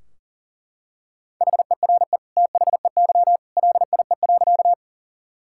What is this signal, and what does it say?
Morse code, heard as a keyed tone, about 40 wpm, 700 Hz. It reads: HERE THEY LIE.
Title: Sentences from Top 1600 Words 40wpm